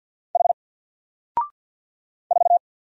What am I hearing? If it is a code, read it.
Morse code, heard as a keyed tone, about 47 wpm, 700 Hz. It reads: H 4
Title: Single Letter-Number Alt 50wpm